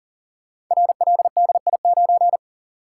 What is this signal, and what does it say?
Morse code, heard as a keyed tone, about 39 wpm, 700 Hz. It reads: RLDI9
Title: Code Groups - 5 Letter:Numbers - 40wpm